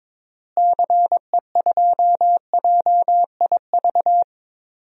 Morse code, heard as a keyed tone, about 22 wpm, 700 Hz. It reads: CE2JIV
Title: DXCC-100 Call Signs - 6 Characters - Repeated 3x - 22wpm